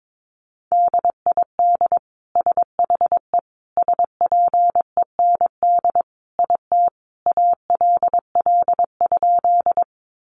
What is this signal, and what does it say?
Morse code, heard as a keyed tone, about 22 wpm, 700 Hz. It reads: DID SHE SPEND IT ALL?
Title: Sentences from Top 800 Words 22wpm